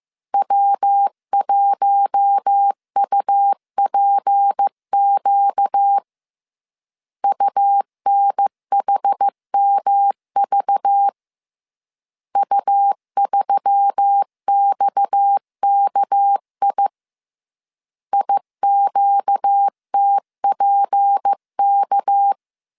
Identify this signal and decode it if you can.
Morse code, heard as a keyed tone, about 15 wpm, 800 Hz. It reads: W1UPQ UNHMV U3XKI IQTPK